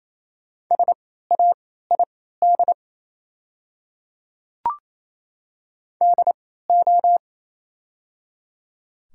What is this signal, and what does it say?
Morse code, heard as a keyed tone, about 28 wpm, spaced as for 16 wpm, 690 Hz. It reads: SAID DO
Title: 3x Character Spacing - Top 500 Words - 28wpm